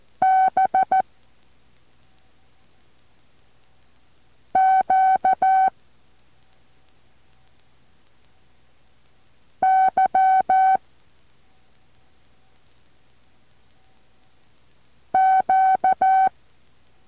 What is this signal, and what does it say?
Morse code, heard as a keyed tone, about 14 wpm, 760 Hz. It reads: B Q Y Q